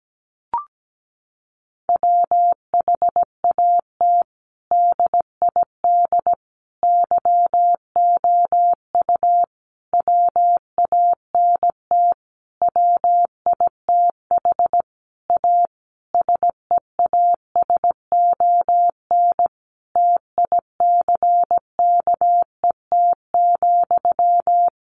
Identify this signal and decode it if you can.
Morse code, heard as a keyed tone, about 17 wpm, 700 Hz. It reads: WHAT DID YOU WANT WITH A SEASON TICKET,
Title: Sentences from Top 1400 Words 17wpm